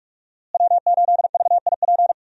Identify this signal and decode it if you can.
Morse code, heard as a keyed tone, about 46 wpm, 700 Hz. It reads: W8VIP
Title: USA Call Signs - 5 Characters - Repeated 2x - 45wpm